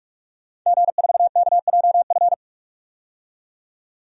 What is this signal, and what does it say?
Morse code, heard as a keyed tone, about 43 wpm, 700 Hz. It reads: G4KJF
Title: DXCC-100 Call Signs - 5 Characters - 45wpm